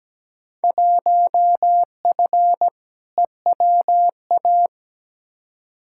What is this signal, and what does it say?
Morse code, heard as a keyed tone, about 17 wpm, 700 Hz. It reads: 1F EWA
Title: Field Day Exchanges Encore - 17wpm